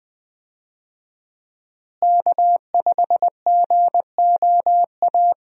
Morse code, heard as a keyed tone, about 21 wpm, 700 Hz. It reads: K5GOA
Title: USA Call Signs - 5 Characters - Repeated 3x - 20wpm